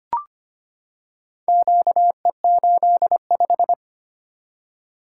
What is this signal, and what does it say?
Morse code, heard as a keyed tone, about 25 wpm, 700 Hz. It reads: QE85
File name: practice / 25wpm / Four Letter-Number Combinations - 25wpm